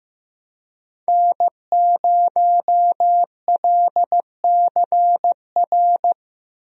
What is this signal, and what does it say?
Morse code, heard as a keyed tone, about 15 wpm, 700 Hz. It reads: N0LCR
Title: USA Call Signs - 5 Characters - Repeated 2x - 15wpm